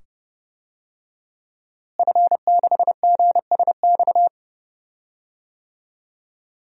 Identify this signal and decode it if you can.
Morse code, heard as a keyed tone, about 30 wpm, 700 Hz. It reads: F6GSX
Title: DXCC-100 Call Signs - 5 Characters - Repeated 2x - 30wpm